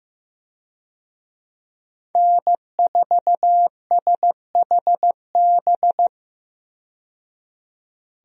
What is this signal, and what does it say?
Morse code, heard as a keyed tone, about 15 wpm, 700 Hz. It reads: N4SHB
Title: USA Call Signs - 5 Characters - 15wpm